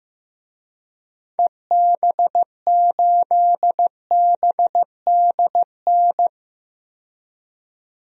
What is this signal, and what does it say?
Morse code, heard as a keyed tone, about 15 wpm, 700 Hz. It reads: EB8BDN
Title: DXCC-100 Call Signs - 6 Characters - 15wpm